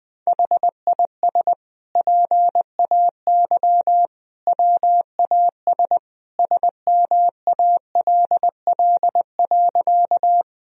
Morse code, heard as a keyed tone, about 20 wpm, 700 Hz. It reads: HIS PAY WAS SMALL.
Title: Sentences from Top 1000 Words - Review Part 1 - 20wpm